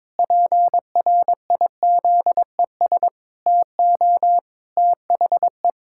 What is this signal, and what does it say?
Morse code, heard as a keyed tone, about 22 wpm, 700 Hz. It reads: PRIZES TO THE
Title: Pangram Challenge v2 - 22wpm